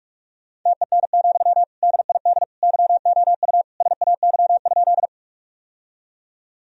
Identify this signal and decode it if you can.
Morse code, heard as a keyed tone, about 41 wpm, 700 Hz. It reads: TEN, DID YOU SAY?
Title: Sentences from Top 1000 Words - Review Part 2 - 45wpm